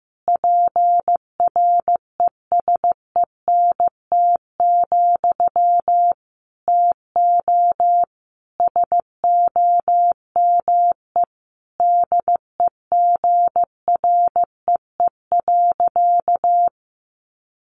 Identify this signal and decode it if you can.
Morse code, heard as a keyed tone, about 15 wpm, 700 Hz. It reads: PRESENT, TO SOME DEGREE.
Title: Sentences from Top 1500 Words 15wpm